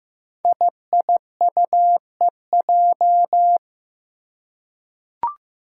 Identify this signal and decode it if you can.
Morse code, heard as a keyed tone, about 15 wpm, 700 Hz. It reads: IIUEJ
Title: Five Challenging Letter-Number Combinations 15wpm